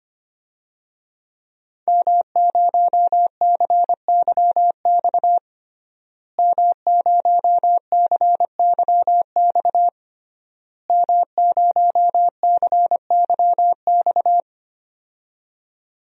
Morse code, heard as a keyed tone, about 25 wpm, 700 Hz. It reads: M0CYX M0CYX M0CYX
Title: DXCC-100 Call Signs - 5 Characters - Repeated 3x - 25wpm